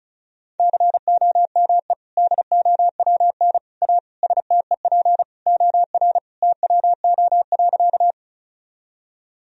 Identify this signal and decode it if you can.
Morse code, heard as a keyed tone, about 34 wpm, 700 Hz. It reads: COME DOWN A STEP OR TWO.
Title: Sentences from Top 1000 Words - Review Part 2 - 35wpm